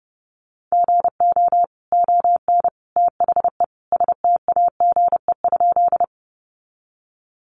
Morse code, heard as a keyed tone, about 30 wpm, 700 Hz. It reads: GO ON THE STAGE?
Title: Sentences from Top 1200 Words 30wpm